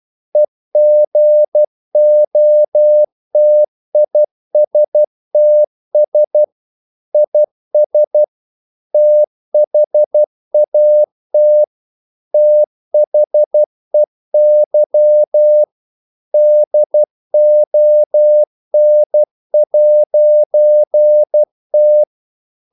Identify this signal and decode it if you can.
Morse code, heard as a keyed tone, about 12 wpm, 600 Hz. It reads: EGOTISTS IS THAT THEY DON'T
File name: CW / Practice / EnglishQuotations0312